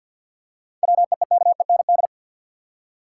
Morse code, heard as a keyed tone, about 50 wpm, 700 Hz. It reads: WEEKEND
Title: Seven Letter Words - 50wpm